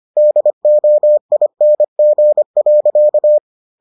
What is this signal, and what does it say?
Morse code, heard as a keyed tone, about 25 wpm, 600 Hz. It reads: DOING.